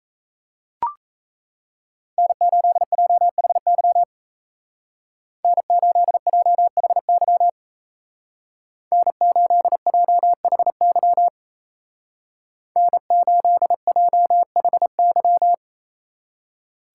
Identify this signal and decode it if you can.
Morse code, heard as a keyed tone, about 36 wpm, 700 Hz. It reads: N8JHY N8JHY N8JHY N8JHY